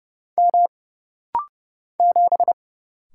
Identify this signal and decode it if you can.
Morse code, heard as a keyed tone, about 30 wpm, 700 Hz. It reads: M 7
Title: Single Letter-Number Alt - Rapid-Fire 30wpm